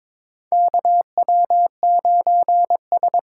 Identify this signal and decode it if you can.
Morse code, heard as a keyed tone, about 23 wpm, 700 Hz. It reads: KW9S
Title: USA Callsign DE Callsign - 4 Characters - 22wpm